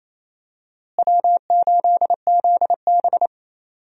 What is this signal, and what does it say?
Morse code, heard as a keyed tone, about 28 wpm, 700 Hz. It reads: W8ZB
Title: USA Call Signs - 4 Characters - Repeated 3x - 28wpm